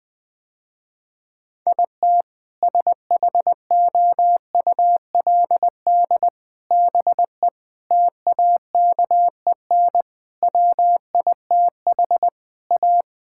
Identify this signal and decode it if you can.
Morse code, heard as a keyed tone, about 20 wpm, 700 Hz. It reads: IT SHOULD BE TAKEN WITH A
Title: Sentences with 8-Word Phrases - Spoken Beginning - 20wpm